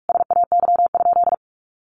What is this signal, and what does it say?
Morse code, heard as a keyed tone, about 44 wpm, 700 Hz. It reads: SAY?
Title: Sentences from Top 500 Words 45wpm